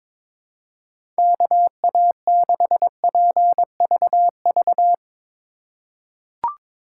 Morse code, heard as a keyed tone, about 22 wpm, 700 Hz. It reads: KA6PVV
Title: USA Call Signs - 3x Character Spacing - 22wpm